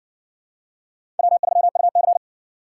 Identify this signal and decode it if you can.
Morse code, heard as a keyed tone, about 56 wpm, 700 Hz. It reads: W3UC